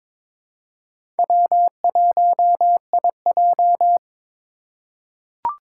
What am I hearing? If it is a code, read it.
Morse code, heard as a keyed tone, about 22 wpm, 700 Hz. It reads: W1IJ
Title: USA Call Signs - 2x Character Spacing - 22wpm